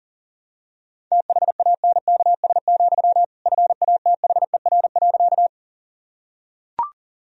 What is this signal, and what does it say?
Morse code, heard as a keyed tone, about 37 wpm, 700 Hz. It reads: THANKS, FATHER.
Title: Sentences from Top 1000 Words - Review Part 3 - 40wpm